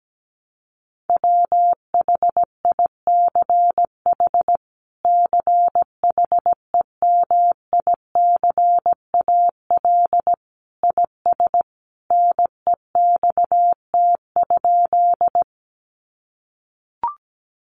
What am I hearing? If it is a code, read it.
Morse code, heard as a keyed tone, about 17 wpm, 700 Hz. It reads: WHICH CHEMICAL IS NEXT?
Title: Sentences from Top 1200 Words 17wpm